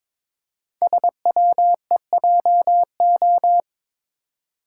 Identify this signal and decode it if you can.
Morse code, heard as a keyed tone, about 22 wpm, 700 Hz. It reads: SWEJO
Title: Code Groups - 5 Letters - 22wpm